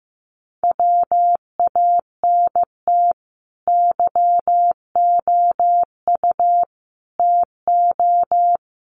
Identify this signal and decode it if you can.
Morse code, heard as a keyed tone, about 15 wpm, 700 Hz. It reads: WANT YOU TO
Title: Sets of 5 Words Encore 15wpm